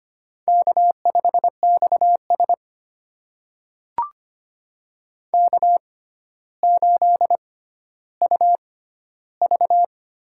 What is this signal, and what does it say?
Morse code, heard as a keyed tone, about 25 wpm, spaced as for 13 wpm, 700 Hz. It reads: K5XS K8UV
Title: USA Call Signs - 4 Characters - 6x Character Spacing - 25wpm